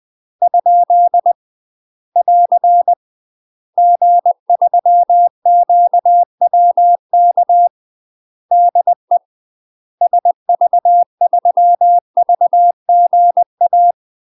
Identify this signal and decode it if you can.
Morse code, heard as a keyed tone, about 20 wpm, 700 Hz. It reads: ? <AR> G3QWK DE SV3VGA
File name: UK MASTER QSOs TRAINING / 20WPM / UK MASTER QSOs 0022